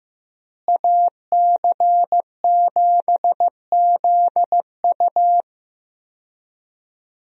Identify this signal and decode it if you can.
Morse code, heard as a keyed tone, about 15 wpm, 700 Hz. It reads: AC7ZU
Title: USA Call Signs - 5 Characters - Repeated 3x - 15wpm